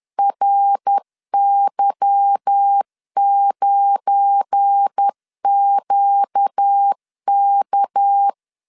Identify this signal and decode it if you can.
Morse code, heard as a keyed tone, about 10 wpm, 800 Hz. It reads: RY9QK